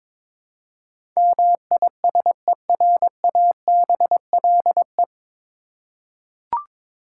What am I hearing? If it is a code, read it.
Morse code, heard as a keyed tone, about 22 wpm, 700 Hz. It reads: MISERABLE
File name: 9 Letter Words - First 3 Letters Given - 22wpm